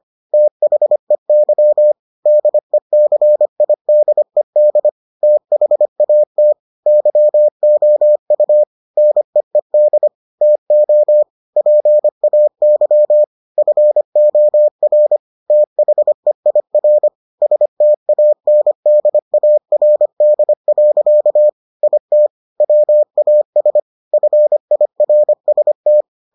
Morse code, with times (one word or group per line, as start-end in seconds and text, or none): 0.3-1.9 THEY
2.3-4.9 DECIDED
5.2-6.5 THAT
6.9-8.6 YOU
9.0-10.1 NEED
10.4-11.2 TO
11.6-13.2 PAY
13.6-15.2 FOR
15.5-17.1 THEIR
17.4-21.5 STANDARD.
21.8-22.3 IT
22.6-23.8 WAS
24.1-26.0 FIRST